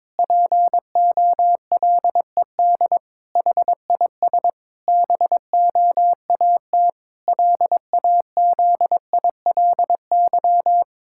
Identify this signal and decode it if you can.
Morse code, heard as a keyed tone, about 22 wpm, 700 Hz. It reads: POLED HIS BOAT LAZILY
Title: Pangram Challenge v2 - 22wpm